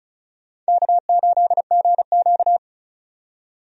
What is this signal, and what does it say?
Morse code, heard as a keyed tone, about 36 wpm, 700 Hz. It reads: K8GQ
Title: USA Call Signs - 4 Characters - 35wpm